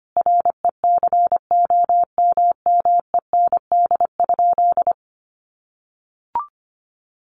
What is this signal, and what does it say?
Morse code, heard as a keyed tone, about 25 wpm, 700 Hz. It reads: RECOMMEND?
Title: Sentences from Top 1200 Words 25wpm